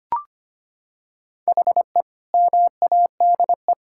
Morse code, heard as a keyed tone, about 25 wpm, 700 Hz. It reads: HE MADE
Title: Sentences from Top 1000 Words - Review Part 6 - 25wpm